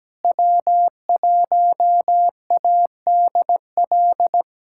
Morse code, heard as a keyed tone, about 17 wpm, 700 Hz. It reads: W1ADL
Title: USA Callsign DE Callsign - 5 Characters - 17wpm